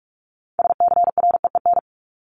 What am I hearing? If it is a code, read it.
Morse code, heard as a keyed tone, about 44 wpm, 700 Hz. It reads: SCREEN
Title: Top 751-1000 Words 45wpm